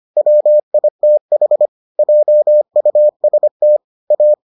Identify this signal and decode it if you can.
Morse code, heard as a keyed tone, about 25 wpm, 600 Hz. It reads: WITH JUST A